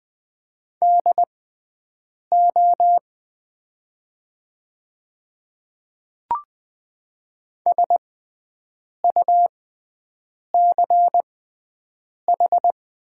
Morse code, heard as a keyed tone, about 20 wpm, spaced as for 7 wpm, 700 Hz. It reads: DO SUCH
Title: 6x Character Spacing - Top 200 Words - 20wpm